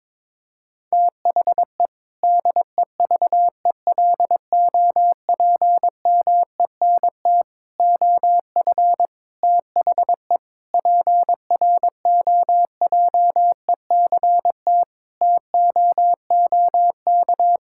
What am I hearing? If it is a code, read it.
Morse code, heard as a keyed tone, about 22 wpm, 700 Hz. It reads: THE DEVELOPMENT OF THE PROJECT TOOK
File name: practice / 22wpm / Sentences with 3-Word Phrases - 22wpm